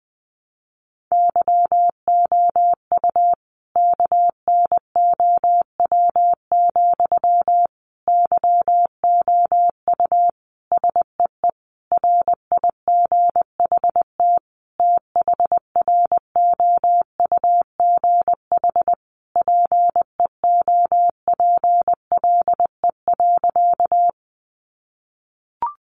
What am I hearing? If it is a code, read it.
Morse code, heard as a keyed tone, about 20 wpm, 700 Hz. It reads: YOU KNOW, YOU SEE RIGHT THROUGH PEOPLE.
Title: Sentences from Top 200 Words 20wpm